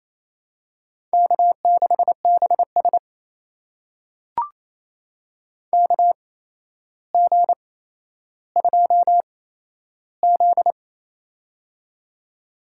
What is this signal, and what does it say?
Morse code, heard as a keyed tone, about 28 wpm, spaced as for 12 wpm, 700 Hz. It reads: K6BS KG2Z
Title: USA Call Signs - 4 Characters - 8x Character Spacing - 28wpm